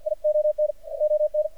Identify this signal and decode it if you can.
Morse code, heard as a keyed tone, about 51 wpm, 610 Hz. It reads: ION ON
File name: archived / CRSnet